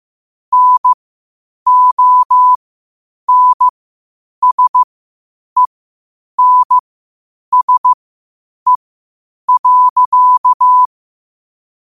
Morse code, heard as a keyed tone, about 15 wpm, spaced as for 8 wpm, 1000 Hz. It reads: NONSENSE.
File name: qotdfiles / 8wpm